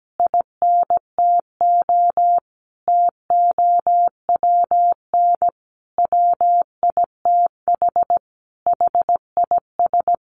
Morse code, heard as a keyed tone, about 17 wpm, 700 Hz. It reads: INTO TOWN WITH HIS
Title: Sentences from Top 1300 Words 17wpm